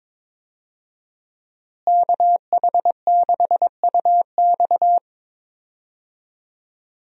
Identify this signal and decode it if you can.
Morse code, heard as a keyed tone, about 22 wpm, 700 Hz. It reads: KH6UX
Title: USA Call Signs - 5 Characters - Repeated 3x - 22wpm